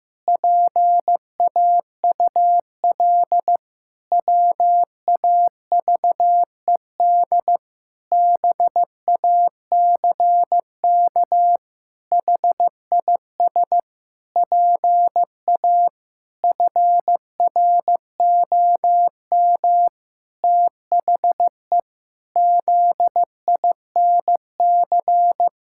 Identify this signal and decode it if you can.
Morse code, heard as a keyed tone, about 16 wpm, 700 Hz. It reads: PAUL WAVED BACK HIS PA FROM THE ZINC
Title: Pangram Challenge v2 - 15wpm